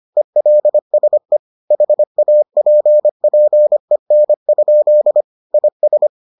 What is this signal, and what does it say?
Morse code, heard as a keyed tone, about 25 wpm, 600 Hz. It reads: ELSE HAPPEN? IS